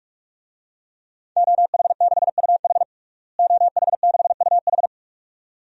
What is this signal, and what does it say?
Morse code, heard as a keyed tone, about 41 wpm, 700 Hz. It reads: OH6UH OH6UH
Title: DXCC-100 Call Signs - 5 Characters - Repeated 2x - 45wpm